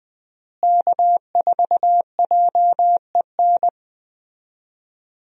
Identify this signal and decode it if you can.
Morse code, heard as a keyed tone, about 20 wpm, 700 Hz. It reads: K4JEN
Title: USA Call Signs - 5 Characters - Repeated 3x - 20wpm